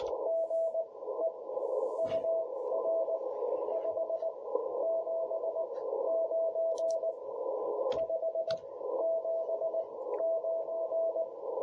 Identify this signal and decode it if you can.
Morse code, heard as a keyed tone, about 18 wpm, spaced as for 13 wpm, 650 Hz, with radio noise. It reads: GE UR 579 579
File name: jh8cbh / 2021cw